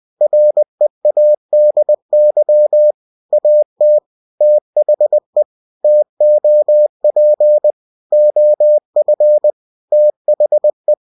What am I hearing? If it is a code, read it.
Morse code, heard as a keyed tone, about 20 wpm, 600 Hz. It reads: READY AT THE TOP OF THE
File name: cw-books / winnie-the-pooh / winnie-20wpm-page-0079